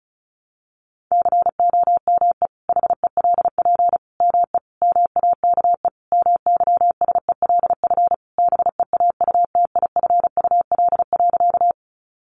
Morse code, heard as a keyed tone, about 34 wpm, 700 Hz. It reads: COME HELP ME MAKE MYSELF BEAUTIFUL.
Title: Sentences from Top 1300 Words 35wpm